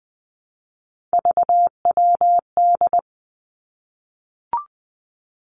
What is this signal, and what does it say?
Morse code, heard as a keyed tone, about 20 wpm, 700 Hz. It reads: VWD
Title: Three Challenging Letter-Number Combinations 20wpm